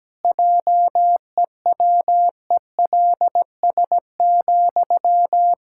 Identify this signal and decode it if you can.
Morse code, heard as a keyed tone, about 17 wpm, 700 Hz. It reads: JEWELS,
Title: Pangram Challenge v2 - 17wpm